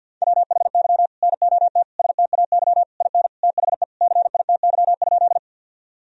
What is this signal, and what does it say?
Morse code, heard as a keyed tone, about 51 wpm, 700 Hz. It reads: WHY NOT STAY IN THE CITY?
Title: Sentences from Top 1000 Words - Review Part 1 - 50wpm